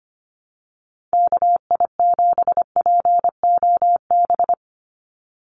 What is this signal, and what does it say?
Morse code, heard as a keyed tone, about 25 wpm, 700 Hz. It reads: KI7POB
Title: USA Call Signs 25wpm